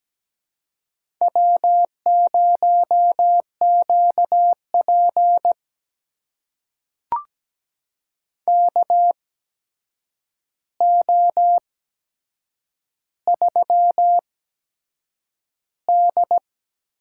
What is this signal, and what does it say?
Morse code, heard as a keyed tone, about 17 wpm, spaced as for 7 wpm, 700 Hz. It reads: W0QP KO3D
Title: USA Call Signs - 4 Characters - 8x Character Spacing - 17wpm